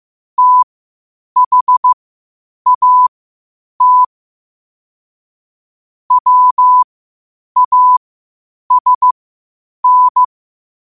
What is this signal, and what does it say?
Morse code, heard as a keyed tone, about 15 wpm, spaced as for 8 wpm, 1000 Hz. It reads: THAT WASN
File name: qotdfiles / 8wpm